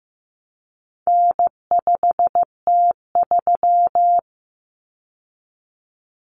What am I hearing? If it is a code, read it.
Morse code, heard as a keyed tone, about 15 wpm, 700 Hz. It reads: N5T3
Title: Common Cut Numbers - 4-Digit 15wpm